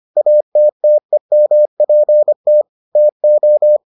Morse code, heard as a keyed tone, about 25 wpm, 600 Hz. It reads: ATTEMPT TO